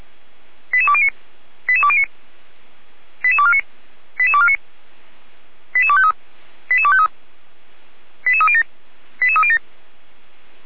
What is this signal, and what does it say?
Five-tone selective call (ZVEI), heard as quick runs of five tones, tones of about 70 ms. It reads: 88209 88209 88249 88249 88253 88253 88307 88307